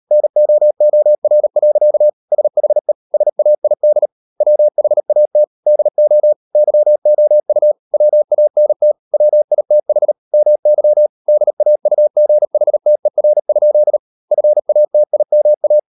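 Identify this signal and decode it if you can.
Morse code, heard as a keyed tone, about 38 wpm, 600 Hz. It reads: NOOR. SHE SAID WHAT DO YOU WANT WITH MY DAUGHTER? FATIMA
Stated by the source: Morse code practice at 38 WPM, sending Ken Follett's novel Never in English, from CW med Gnister.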